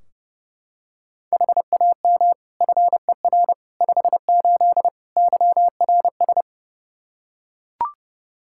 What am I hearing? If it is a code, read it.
Morse code, heard as a keyed tone, about 30 wpm, 700 Hz. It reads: HAM FER 58 YRS
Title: Practice QSO - Exchange 3 - More Info - 30wpm